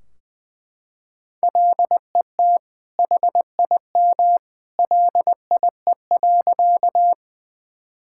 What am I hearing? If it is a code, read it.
Morse code, heard as a keyed tone, about 21 wpm, 700 Hz. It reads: LET HIM LIE.
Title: Sentences from Top 1600 Words 20wpm